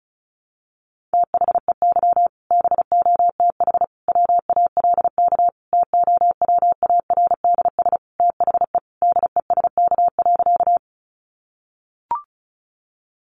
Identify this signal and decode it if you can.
Morse code, heard as a keyed tone, about 34 wpm, 700 Hz. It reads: THEY BOTH WALK TOWARDS THE DESK.
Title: Additional Easy Phrases 35wpm